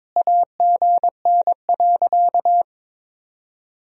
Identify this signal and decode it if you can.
Morse code, heard as a keyed tone, about 22 wpm, 700 Hz. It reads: AGN.
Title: Sentences from Top 300 Words and Abbreviations 22wpm